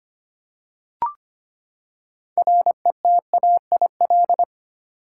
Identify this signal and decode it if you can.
Morse code, heard as a keyed tone, about 25 wpm, 700 Hz. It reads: RETAIL
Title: Practice QSO - Exchange 3 - More Info - 25wpm